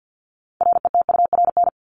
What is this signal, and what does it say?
Morse code, heard as a keyed tone, about 50 wpm, 700 Hz. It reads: RETURN
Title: Top 751-1000 Words 50wpm